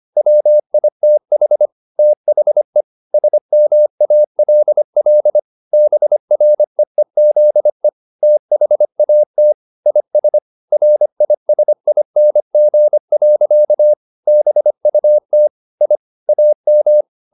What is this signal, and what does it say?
Morse code, heard as a keyed tone, about 25 wpm, 600 Hz. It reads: WITH THE SMALL BREEZE THAT IS RISING. BUT I AM
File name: the old man and the sea part 1 25wpm